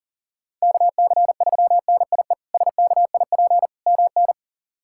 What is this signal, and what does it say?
Morse code, heard as a keyed tone, about 40 wpm, 700 Hz. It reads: KC3NIE SKIP MN
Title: K1USN SST QSO Practice - 40wpm